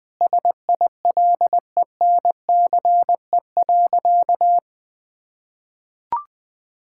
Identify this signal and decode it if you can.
Morse code, heard as a keyed tone, about 20 wpm, 700 Hz. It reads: SILENCE.
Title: Sentences with 3-Word Phrases - Spoken Beginning - 20wpm